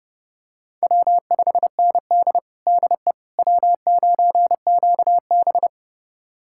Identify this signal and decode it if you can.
Morse code, heard as a keyed tone, about 31 wpm, 700 Hz. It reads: W5ND DE W9QB
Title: USA Callsign DE Callsign - 4 Characters - 30wpm